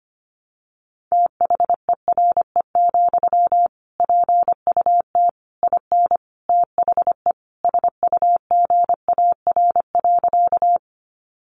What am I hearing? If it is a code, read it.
Morse code, heard as a keyed tone, about 25 wpm, 700 Hz. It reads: THERE, PUT IN THE SUGAR.
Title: Sentences from Top 1200 Words 25wpm